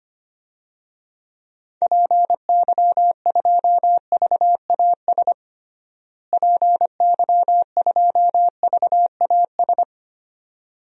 Morse code, heard as a keyed tone, about 25 wpm, 700 Hz. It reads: PY2VAS PY2VAS
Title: DXCC-100 Call Signs - 6 Characters - Repeated 2x - 25wpm